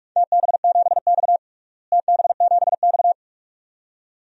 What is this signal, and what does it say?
Morse code, heard as a keyed tone, about 46 wpm, 700 Hz. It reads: TB7X TB7X
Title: DXCC-100 Call Signs - 4 Characters - Repeated 2x - 45wpm